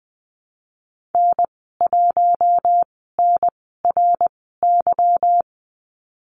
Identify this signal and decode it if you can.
Morse code, heard as a keyed tone, about 19 wpm, spaced as for 14 wpm, 700 Hz. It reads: N1NRY
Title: 173 Lesson 28g - Number 9 - Call Signs 20wpm